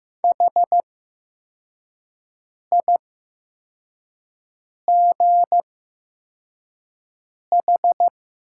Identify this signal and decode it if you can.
Morse code, heard as a keyed tone, about 15 wpm, spaced as for 4 wpm, 700 Hz. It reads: HIGH
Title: 8x Character Spacing - Top 200 Words - 15wpm